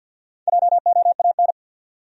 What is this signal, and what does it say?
Morse code, heard as a keyed tone, about 54 wpm, 700 Hz. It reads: JOAN
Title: USA Common Names - Max 4 Letters - 50wpm